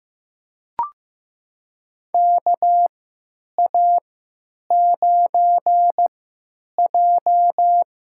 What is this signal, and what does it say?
Morse code, heard as a keyed tone, about 15 wpm, spaced as for 9 wpm, 700 Hz. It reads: KA9J
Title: USA Call Signs - 3x Character Spacing - 15wpm